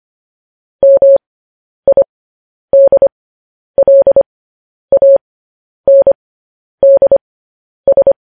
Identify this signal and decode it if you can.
Morse code, heard as a keyed tone, about 25 wpm, spaced as for 10 wpm, 550 Hz. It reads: MIDLANDS
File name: podcast / News headlines-1x-25x10